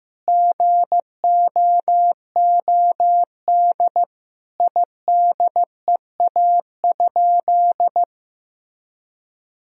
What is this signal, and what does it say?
Morse code, heard as a keyed tone, about 15 wpm, 700 Hz. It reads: GOOD IDEA?
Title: Sentences from Top 1000 Words - Review Part 4 - 15wpm